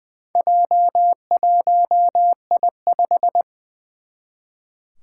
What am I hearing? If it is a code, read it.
Morse code, heard as a keyed tone, about 21 wpm, 700 Hz. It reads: J1I5
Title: Four Challenging Letter-Number Combinations 20wpm